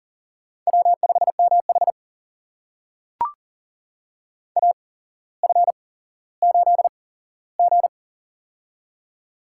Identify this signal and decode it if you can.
Morse code, heard as a keyed tone, about 40 wpm, spaced as for 17 wpm, 700 Hz. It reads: W5MH AF8G
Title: USA Call Signs - 4 Characters - 8x Character Spacing - 40wpm